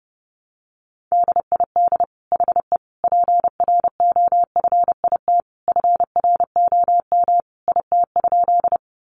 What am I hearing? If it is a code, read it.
Morse code, heard as a keyed tone, about 30 wpm, 700 Hz. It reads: DID HE PROFIT FROM IT?
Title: Sentences from Top 800 Words 30wpm